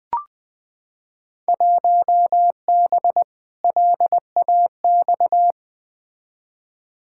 Morse code, heard as a keyed tone, about 21 wpm, 700 Hz. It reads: 1B LAX
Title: Field Day Exchanges - 20wpm